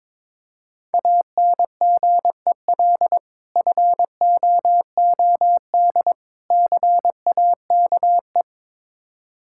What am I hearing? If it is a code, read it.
Morse code, heard as a keyed tone, about 21 wpm, 700 Hz. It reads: ANGEL FOOD CAKE
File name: Difficult Words with Hints - 22wpm